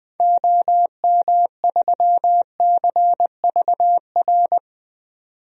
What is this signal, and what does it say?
Morse code, heard as a keyed tone, about 20 wpm, 700 Hz. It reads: OM3CVR
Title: DXCC-100 Call Signs - 6 Characters - 20wpm